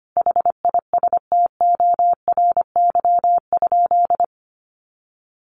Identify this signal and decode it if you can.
Morse code, heard as a keyed tone, about 25 wpm, 700 Hz. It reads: HISTORY?
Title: Sentences from Top 600 Words 25wpm